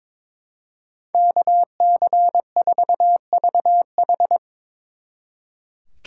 Morse code, heard as a keyed tone, about 24 wpm, 700 Hz. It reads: KC4VH
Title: Speed Racing - USA 2x2 Call Signs 22wpm